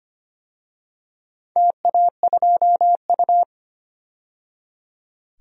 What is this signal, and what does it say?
Morse code, heard as a keyed tone, about 25 wpm, 690 Hz. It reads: TA2U
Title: DXCC-100 Call Signs - 4 Characters - 25wpm